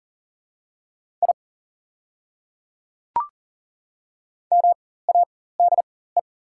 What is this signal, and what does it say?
Morse code, heard as a keyed tone, about 40 wpm, spaced as for 19 wpm, 700 Hz. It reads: I MADE
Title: 4x Character Spacing - Top 200 Words - 40wpm